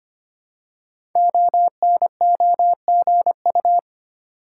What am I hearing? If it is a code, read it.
Morse code, heard as a keyed tone, about 25 wpm, 700 Hz. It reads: ONOGU
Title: Code Groups - 5 Letters - 25wpm